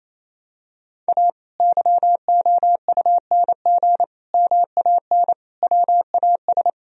Morse code, heard as a keyed tone, about 27 wpm, 700 Hz. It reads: A YOUNG MAN WAS
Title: Sentences from Top 1000 Words - Review Part 3 - 28wpm